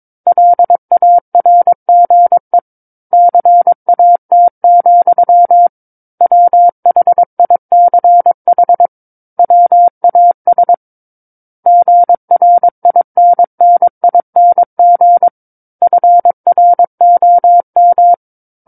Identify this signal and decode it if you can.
Morse code, heard as a keyed tone, about 22 wpm, 700 Hz. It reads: LARGE CAT, WHICH WAS GRINNING FROM
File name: 06-Chapter-06 22 WPM